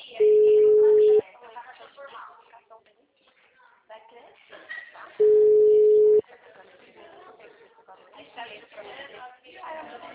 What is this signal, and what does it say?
Ringback tone (Europe, 425 Hz), with speech.